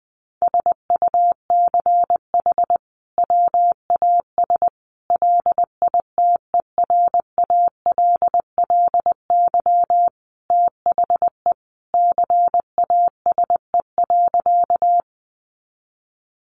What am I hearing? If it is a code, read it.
Morse code, heard as a keyed tone, about 20 wpm, 700 Hz. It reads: SUCH WAS LITERALLY THE CASE.
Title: Sentences from Top 1400 Words 20wpm